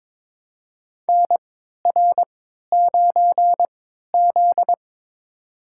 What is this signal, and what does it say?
Morse code, heard as a keyed tone, about 22 wpm, spaced as for 13 wpm, 700 Hz. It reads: NR9Z
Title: USA Call Signs - 4 Characters - 3x Character Spacing - 22wpm